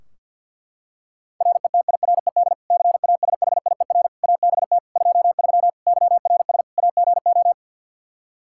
Morse code, heard as a keyed tone, about 47 wpm, 700 Hz. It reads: RETIRED CASHIER ABT 23 YRS AGO
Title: Practice QSO - Exchange 3 - More Info - 50wpm